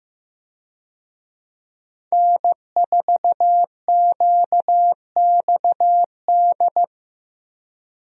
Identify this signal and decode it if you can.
Morse code, heard as a keyed tone, about 15 wpm, 700 Hz. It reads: N4QXD
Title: USA Call Signs - 5 Characters - 15wpm